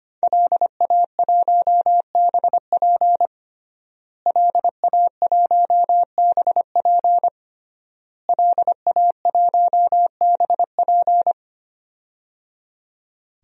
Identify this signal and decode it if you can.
Morse code, heard as a keyed tone, about 25 wpm, 700 Hz. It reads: LA1BP LA1BP LA1BP
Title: DXCC-100 Call Signs - 5 Characters - Repeated 3x - 25wpm